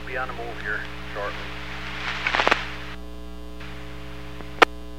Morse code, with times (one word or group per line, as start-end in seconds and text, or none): none